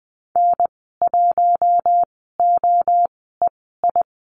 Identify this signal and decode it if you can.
Morse code, heard as a keyed tone, about 20 wpm, spaced as for 14 wpm, 700 Hz. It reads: N1OEI
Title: 042 Lesson 8g - Number 1 - Call Signs 20wpm